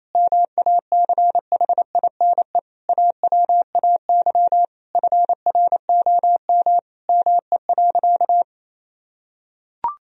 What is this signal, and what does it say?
Morse code, heard as a keyed tone, about 28 wpm, 700 Hz. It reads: MACHINE AWAY FROM ME.
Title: Sentences from Top 1000 Words - Review Part 2 - 28wpm